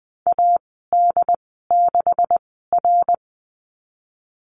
Morse code, heard as a keyed tone, about 20 wpm, spaced as for 15 wpm, 700 Hz. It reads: AD6R
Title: 147 Lesson 24g - Letter y - Call Signs 20wpm